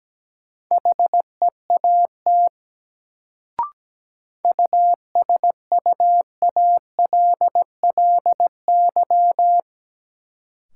Morse code, heard as a keyed tone, about 17 wpm, 700 Hz. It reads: HEAT USUALLY
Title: Sentences from Top 1000 Words - Review Part 3 - New Word Drill - 17wpm